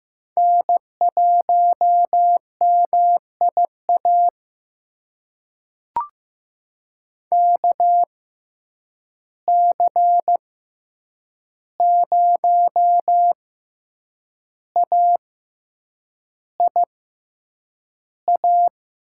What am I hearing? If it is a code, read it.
Morse code, heard as a keyed tone, about 15 wpm, spaced as for 7 wpm, 700 Hz. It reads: N1MIA KC0AIA
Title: USA Call Signs - 6x Character Spacing - 15wpm